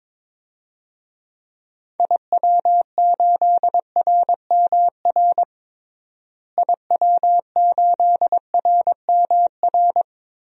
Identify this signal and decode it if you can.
Morse code, heard as a keyed tone, about 22 wpm, 700 Hz. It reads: IW8RMR IW8RMR
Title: DXCC-100 Call Signs - 6 Characters - Repeated 3x - 22wpm